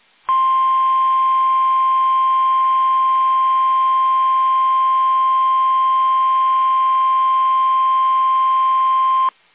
Emergency Alert System (SAME) tones (the 1050 Hz attention tone). It is heard from 0.2 to 9.4 s.